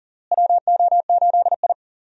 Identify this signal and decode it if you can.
Morse code, heard as a keyed tone, about 40 wpm, 700 Hz. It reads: WO8I
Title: USA Call Signs - 4 Characters - Repeated 3x - 40wpm